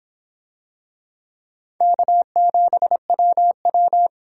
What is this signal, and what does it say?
Morse code, heard as a keyed tone, about 23 wpm, 700 Hz. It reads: K7WW